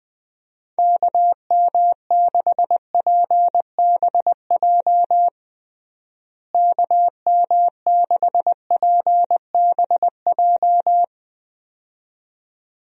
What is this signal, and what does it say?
Morse code, heard as a keyed tone, about 21 wpm, 700 Hz. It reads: KM6PBJ KM6PBJ
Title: USA Call Signs - 6 Characters - Repeated 2x - 20wpm